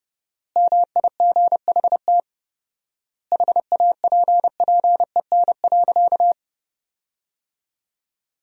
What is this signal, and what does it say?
Morse code, heard as a keyed tone, about 30 wpm, 700 Hz. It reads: MIGHT HAPPEN.
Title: 4x Word-Spacing - Sentences from Top 400 Words - 30wpm